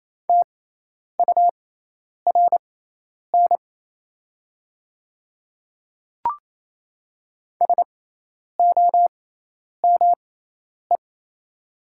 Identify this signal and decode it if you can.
Morse code, heard as a keyed tone, about 27 wpm, spaced as for 10 wpm, 700 Hz. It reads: TURN SOME
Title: 6x Character Spacing - Top 200 Words - 28wpm